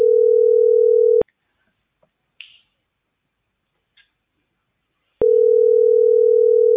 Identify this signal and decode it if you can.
Ringback tone (North America), 440 + 480 Hz.